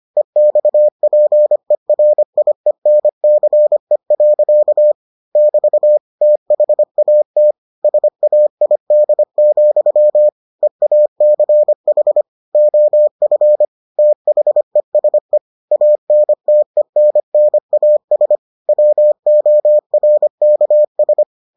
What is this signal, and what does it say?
Morse code, heard as a keyed tone, about 25 wpm, 600 Hz. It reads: EXPERIENCE. <BT> THAT SAID, EACH OF THESE ANTENNAS WORKS